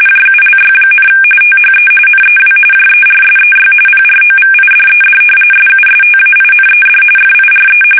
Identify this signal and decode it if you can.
RTTY, about 45.45 baud. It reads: JULY, THE NEW JERSEY PROVIDED SUSTAINED FIRE
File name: fsk850 / battleshipnj